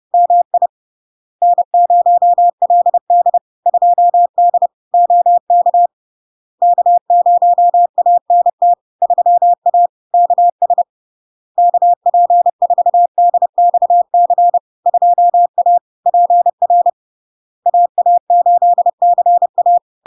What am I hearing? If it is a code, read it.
Morse code, heard as a keyed tone, about 30 wpm, 700 Hz. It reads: MI N0LD 2D OK K0ANT 3A KS KP4DXC 2A PR AA8CA
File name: LARK-FD-2022-Log-in-CW-30wpm-700-Hz